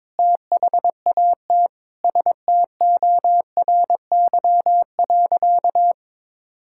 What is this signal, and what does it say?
Morse code, heard as a keyed tone, about 22 wpm, 700 Hz. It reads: THAT STORY.
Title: Sentences from Top 1000 Words - Review Part 5 - 22wpm